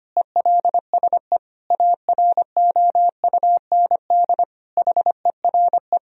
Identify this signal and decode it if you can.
Morse code, heard as a keyed tone, about 25 wpm, 700 Hz. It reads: ELSE AROUND HERE
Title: Sentences from Top 1000 Words - Review Part 3 - 25wpm